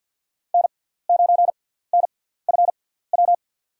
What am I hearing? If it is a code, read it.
Morse code, heard as a keyed tone, about 50 wpm, spaced as for 17 wpm, 700 Hz. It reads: N9NFW
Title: USA Call Signs - 6x Character Spacing - 50wpm